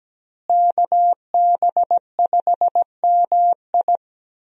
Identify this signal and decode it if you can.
Morse code, heard as a keyed tone, about 17 wpm, 700 Hz. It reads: KB5MI
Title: USA Call Signs - 5 Characters - Repeated 3x - 17wpm